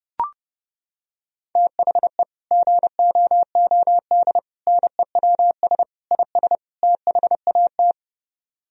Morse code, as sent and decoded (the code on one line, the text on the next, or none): - .... / --. --- --- -.. / -. .-- ... / .. ... / - .... .- -
THE GOOD NEWS IS THAT